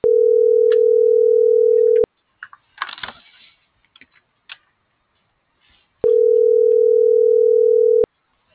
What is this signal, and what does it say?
Ringback tone (North America), 440 + 480 Hz.